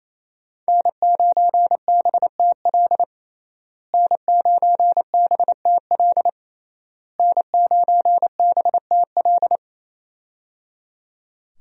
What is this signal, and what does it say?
Morse code, heard as a keyed tone, about 27 wpm, 700 Hz. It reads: N9BTL N9BTL N9BTL
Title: USA Call Signs - 5 Characters - Repeated 3x - 28wpm